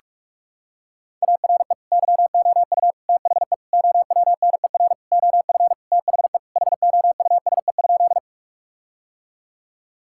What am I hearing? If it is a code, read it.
Morse code, heard as a keyed tone, about 43 wpm, 700 Hz. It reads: ARE YOU THE OWNER OF THE HOUSE?
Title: Sentences from Top 1600 Words 45wpm